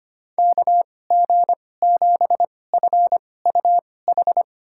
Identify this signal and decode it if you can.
Morse code, heard as a keyed tone, about 25 wpm, spaced as for 18 wpm, 700 Hz. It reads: KG7FUH
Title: USA Call Signs - 2x Character Spacing - 25wpm